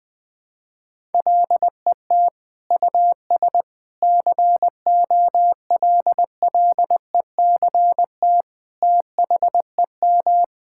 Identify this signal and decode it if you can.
Morse code, heard as a keyed tone, about 20 wpm, 700 Hz. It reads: LET US COLLECT THEM
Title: Sentences from Top 1600 Words 20wpm